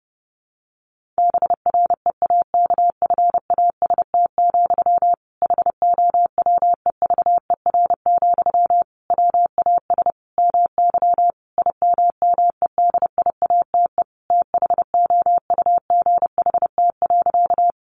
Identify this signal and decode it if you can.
Morse code, heard as a keyed tone, about 30 wpm, 700 Hz. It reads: BREAKFAST, HOWEVER, WAS MY IMMEDIATE THOUGHT.
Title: Sentences from Top 1500 Words 30wpm